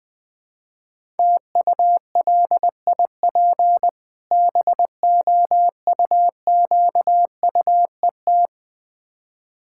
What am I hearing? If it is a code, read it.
Morse code, heard as a keyed tone, about 20 wpm, 690 Hz. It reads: TULIP BOUQUET
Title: Difficult Words with Hints - 20wpm